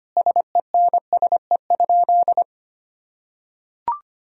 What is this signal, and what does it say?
Morse code, heard as a keyed tone, about 25 wpm, 690 Hz. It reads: SENSE?
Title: Sentences from Top 1000 Words - Review Part 4 - 25wpm